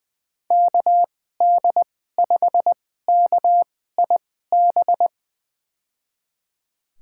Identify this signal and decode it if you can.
Morse code, heard as a keyed tone, about 20 wpm, spaced as for 15 wpm, 700 Hz. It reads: KD5KIB
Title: USA Call Signs - 6 Characters - 2x Character Spacing - 20wpm